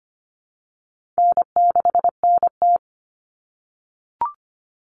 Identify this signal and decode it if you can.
Morse code, heard as a keyed tone, about 25 wpm, 700 Hz. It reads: N6NT
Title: Common Cut Numbers - 4-Digit 25wpm